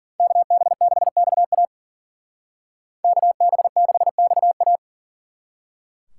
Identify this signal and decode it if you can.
Morse code, heard as a keyed tone, about 49 wpm, 700 Hz. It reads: KB6XA KB6XA
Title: Speed Racing - USA 2x2 Call Signs 40wpm